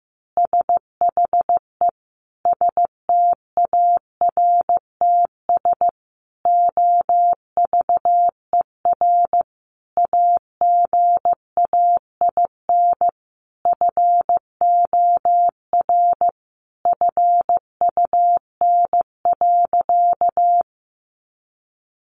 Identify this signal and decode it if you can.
Morse code, heard as a keyed tone, about 15 wpm, 700 Hz. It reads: SHE STARTS OVER AGAIN FOR FUN.
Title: Easy Phrases 15wpm